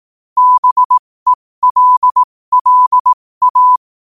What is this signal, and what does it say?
Morse code, heard as a keyed tone, about 18 wpm, spaced as for 15 wpm, 1000 Hz. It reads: BELLA
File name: qotdfiles / 15wpm